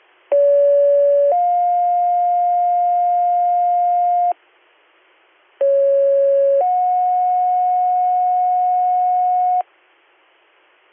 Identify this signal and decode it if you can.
Two-tone page, heard as a one-second tone, then a three-second tone, about 570 then 710 Hz, about 550 then 730 Hz, with noise. It is heard from 0.3 to 9.6 s.